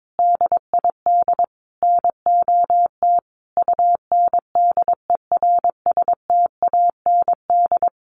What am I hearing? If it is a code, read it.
Morse code, heard as a keyed tone, about 22 wpm, 700 Hz. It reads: DID NOT UNDERSTAND
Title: Sentences from Top 600 Words 22wpm